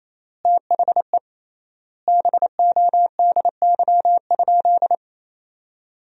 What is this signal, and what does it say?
Morse code, heard as a keyed tone, about 28 wpm, 700 Hz. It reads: THE BODY?
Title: 3x Word-Spacing - Sentences from Top 400 Words - 28wpm